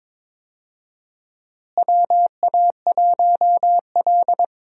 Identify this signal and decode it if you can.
Morse code, heard as a keyed tone, about 22 wpm, 700 Hz. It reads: WA1L
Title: USA Call Signs - 4 Characters - Repeated 2x - 22wpm